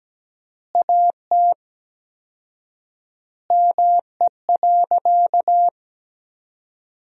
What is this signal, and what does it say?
Morse code, heard as a keyed tone, about 17 wpm, 700 Hz. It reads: AT ME.
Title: 4x Word-Spacing - Sentences from Top 200 Words - 17wpm